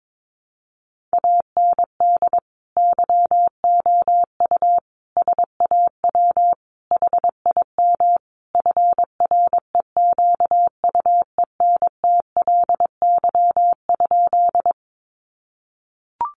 Morse code, heard as a keyed tone, about 22 wpm, 700 Hz. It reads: AND YOU SAW HIM FREQUENTLY?
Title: Sentences from Top 1300 Words 22wpm